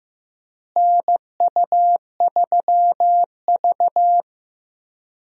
Morse code, heard as a keyed tone, about 15 wpm, 700 Hz. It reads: NU3V
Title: USA Call Signs - 4 Characters - Repeated 3x - 15wpm